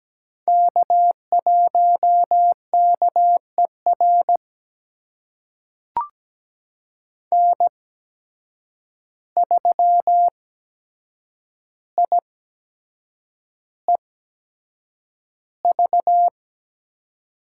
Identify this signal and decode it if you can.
Morse code, heard as a keyed tone, about 17 wpm, spaced as for 7 wpm, 700 Hz. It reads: K1KER N3IEV
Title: USA Call Signs - 5 Characters - 8x Character Spacing - 17wpm